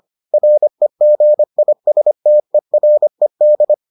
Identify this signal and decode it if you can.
Morse code, heard as a keyed tone, about 25 wpm, 600 Hz. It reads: REGISTERED